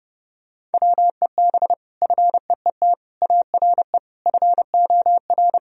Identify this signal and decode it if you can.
Morse code, heard as a keyed tone, about 30 wpm, 700 Hz. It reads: WEB FEET ARE FOR
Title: Sentences from Top 1000 Words - Review Part 4 - 30wpm